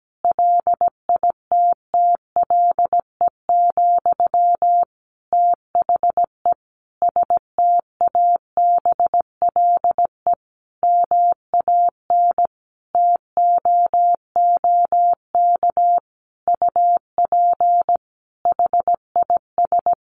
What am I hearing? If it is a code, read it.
Morse code, heard as a keyed tone, about 17 wpm, 700 Hz. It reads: LITTLE, THE STABLE MAN TOOK UP HIS
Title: Sentences from Top 1500 Words 17wpm